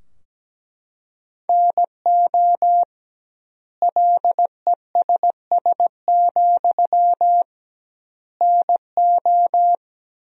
Morse code, heard as a keyed tone, about 17 wpm, 700 Hz. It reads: NO LESS, NO
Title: 2x Word-Spacing - Sentences from Top 400 Words - 17wpm